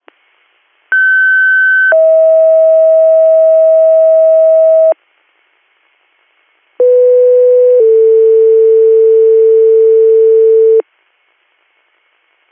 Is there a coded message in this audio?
Yes: two-tone page from 0.9 to 10.8 s, heard as a one-second tone, then a three-second tone, about 1550 then 640 Hz, about 500 then 450 Hz.